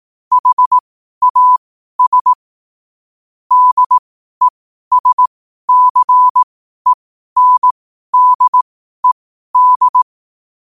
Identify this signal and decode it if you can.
Morse code, heard as a keyed tone, about 18 wpm, spaced as for 12 wpm, 1000 Hz. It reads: HAS DESCENDED
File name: qotdfiles / 12wpm